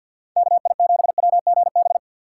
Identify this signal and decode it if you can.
Morse code, heard as a keyed tone, about 58 wpm, 700 Hz. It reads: KI7WGB
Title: Speed Racing - USA Call Signs 50wpm